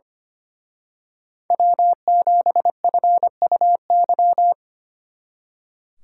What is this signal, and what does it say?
Morse code, heard as a keyed tone, about 28 wpm, 700 Hz. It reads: W7FUY
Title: Speed Racing - USA 1x3 Call Signs 25wpm